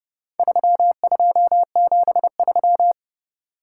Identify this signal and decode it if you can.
Morse code, heard as a keyed tone, about 31 wpm, 700 Hz. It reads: 3273